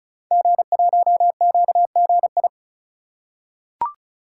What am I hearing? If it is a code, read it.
Morse code, heard as a keyed tone, about 36 wpm, 700 Hz. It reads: G1QGI
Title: Code Groups - 5 Letter:Numbers - 35wpm